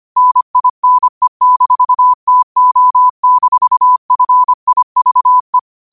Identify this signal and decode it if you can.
Morse code, heard as a keyed tone, about 25 wpm, 1000 Hz. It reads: NINE-TO-FIVE
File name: qotdfiles / 25wpm